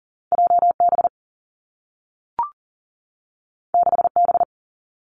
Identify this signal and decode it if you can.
Morse code, heard as a keyed tone, about 41 wpm, 700 Hz. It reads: JB 6B
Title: Two Challenging Letter-Number Combinations 40wpm